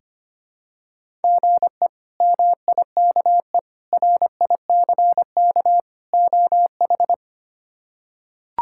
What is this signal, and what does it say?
Morse code, heard as a keyed tone, about 25 wpm, 700 Hz. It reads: GE MIKE RICK OH
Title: K1USN SST QSO Practice - 25wpm